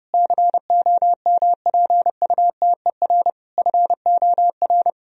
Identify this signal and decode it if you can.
Morse code, heard as a keyed tone, about 30 wpm, 700 Hz. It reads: COMPUTER FOR
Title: Sentences from Top 1000 Words - Review Part 3 - 30wpm